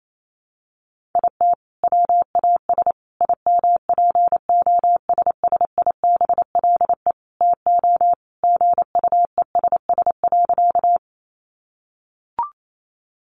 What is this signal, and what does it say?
Morse code, heard as a keyed tone, about 28 wpm, 700 Hz. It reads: IT WAS IMPOSSIBLE TO GUESS.
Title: Sentences from Top 1500 Words 28wpm